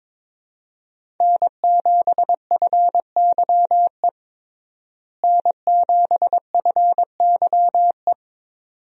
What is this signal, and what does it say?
Morse code, heard as a keyed tone, about 22 wpm, 700 Hz. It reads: N7FYE N7FYE
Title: USA Call Signs - 5 Characters - Repeated 2x - 22wpm